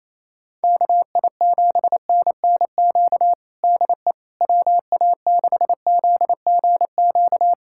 Morse code, heard as a keyed tone, about 28 wpm, 700 Hz. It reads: KI7NNQ DE WA6ZGQ
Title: USA Callsign DE Callsign - 6 Characters - 28wpm